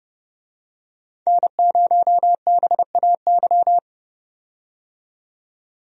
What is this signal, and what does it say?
Morse code, heard as a keyed tone, about 31 wpm, 700 Hz. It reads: N0BAY
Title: USA Call Signs - 5 Characters - 30wpm